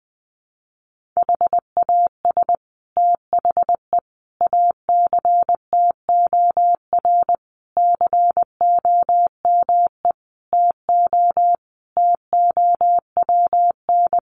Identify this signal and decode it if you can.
Morse code, heard as a keyed tone, about 20 wpm, 700 Hz. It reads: HAS THE ACTOR COME TO TOWN
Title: Sentences from Top 1500 Words 20wpm